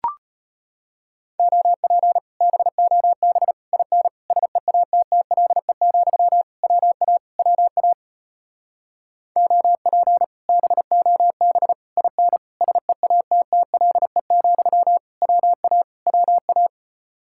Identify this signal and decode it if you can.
Morse code, heard as a keyed tone, about 33 wpm, 700 Hz. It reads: OP BOB IN SEATTLE, WA WA OP BOB IN SEATTLE, WA WA